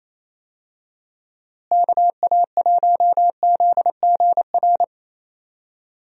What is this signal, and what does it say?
Morse code, heard as a keyed tone, about 28 wpm, 700 Hz. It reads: KA1ZGR
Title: USA Call Signs - 6 Characters - Repeated 3x - 28wpm